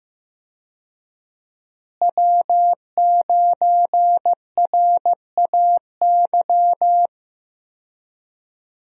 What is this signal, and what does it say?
Morse code, heard as a keyed tone, about 16 wpm, 700 Hz. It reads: W9RAY
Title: USA Call Signs - 5 Characters - Repeated 2x - 15wpm